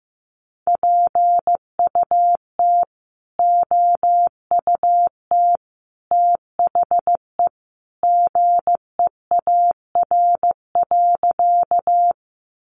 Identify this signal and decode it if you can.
Morse code, heard as a keyed tone, about 15 wpm, 700 Hz. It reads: PUT OUT THE GEAR.
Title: Sentences from Top 1400 Words 15wpm